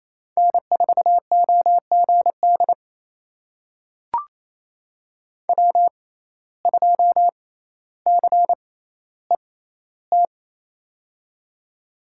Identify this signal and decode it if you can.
Morse code, heard as a keyed tone, about 28 wpm, spaced as for 14 wpm, 700 Hz. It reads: N4OGD W2CET
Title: USA Call Signs - 5 Characters - 6x Character Spacing - 28wpm